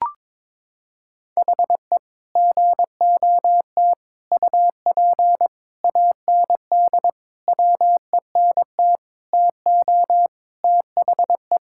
Morse code, heard as a keyed tone, about 22 wpm, 700 Hz. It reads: HE GOT UP AND WENT TO THE
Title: Sentences from Top 1600 Words 22wpm